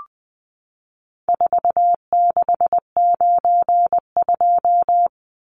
Morse code, heard as a keyed tone, about 20 wpm, 700 Hz. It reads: 4692
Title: Numbers - 4-Digit 20wpm